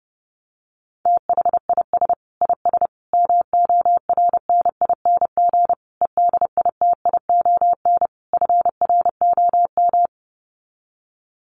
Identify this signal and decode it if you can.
Morse code, heard as a keyed tone, about 30 wpm, 700 Hz. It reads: THIS IS MORNING EDITION FROM
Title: Sets of 5 Words Encore2 30wpm